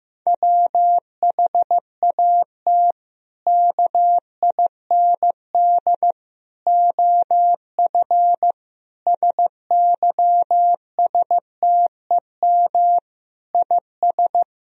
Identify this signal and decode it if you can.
Morse code, heard as a keyed tone, about 15 wpm, 700 Hz. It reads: WHAT KIND OF SYSTEM IS
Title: Sentences from Top 1000 Words - Review Part 3 - 15wpm